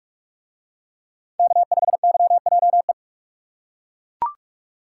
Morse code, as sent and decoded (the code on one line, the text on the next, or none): -.- ..... -.-- .--- .
K5YJE